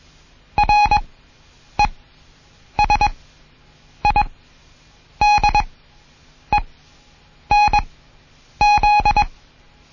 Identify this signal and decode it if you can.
Morse code, heard as a keyed tone, about 21 wpm, spaced as for 8 wpm, 830 Hz. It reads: RESIDENZ
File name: Morsen / Kurs A 20